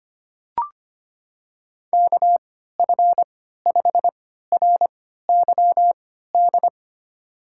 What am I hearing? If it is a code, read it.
Morse code, heard as a keyed tone, about 25 wpm, spaced as for 14 wpm, 700 Hz. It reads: KF5RYD
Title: USA Call Signs - 6 Characters - 3x Character Spacing - 25wpm